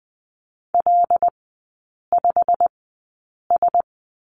Morse code, heard as a keyed tone, about 20 wpm, spaced as for 8 wpm, 700 Hz. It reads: L5S
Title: 205 Lesson 33e - Letter j - Three Characters 20wpm